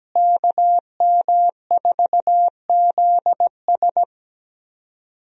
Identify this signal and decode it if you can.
Morse code, heard as a keyed tone, about 17 wpm, 700 Hz. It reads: KM4ZS
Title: USA Call Signs - 5 Characters - Repeated 2x - 17wpm